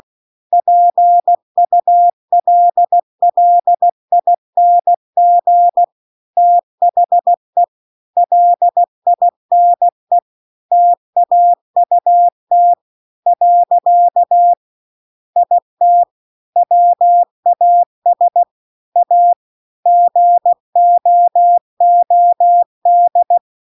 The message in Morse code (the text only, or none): PULLING THE LINE TAUT . IT WAS A GOOD